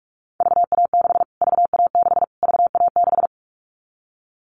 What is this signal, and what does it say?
Morse code, heard as a keyed tone, about 46 wpm, 700 Hz. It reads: VA6 VA6 VA6
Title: Callsign Prefixes for Top 25 DXCC - Repeated 3x - 45wpm